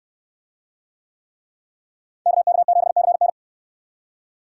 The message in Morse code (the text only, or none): KK7CN